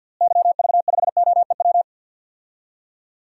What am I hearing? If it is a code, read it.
Morse code, heard as a keyed tone, about 52 wpm, 700 Hz. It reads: YV5OEW
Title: DXCC-100 Call Signs - 6 Characters - Repeated 2x - 50wpm